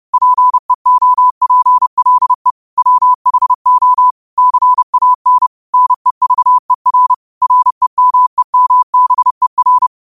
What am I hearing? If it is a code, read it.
Morse code, heard as a keyed tone, about 30 wpm, 1000 Hz. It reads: PEOPLE WHO CAN NEVER REMEMBER
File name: qotdfiles / 30wpm